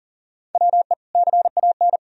Morse code, heard as a keyed tone, about 40 wpm, 700 Hz. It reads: WE CAN
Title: Sentences from Top 1000 Words - Review Part 3 - 40wpm